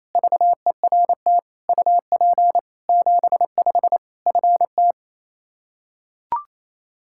Morse code, heard as a keyed tone, about 28 wpm, 700 Hz. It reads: VERT UP 75 FT
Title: Practice QSO - Exchange 2 - Info - 28wpm